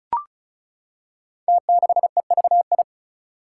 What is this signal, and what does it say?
Morse code, heard as a keyed tone, about 36 wpm, 700 Hz. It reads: T6EVI
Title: Five Challenging Letter-Number Combinations 35wpm